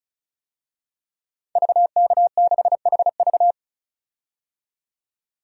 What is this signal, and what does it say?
Morse code, heard as a keyed tone, about 36 wpm, 700 Hz. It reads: VK6HV
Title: DXCC-100 Call Signs - 5 Characters - 35wpm